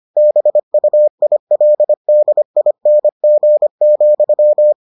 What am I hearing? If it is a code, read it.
Morse code, heard as a keyed tone, about 25 wpm, 600 Hz. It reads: BUILDING,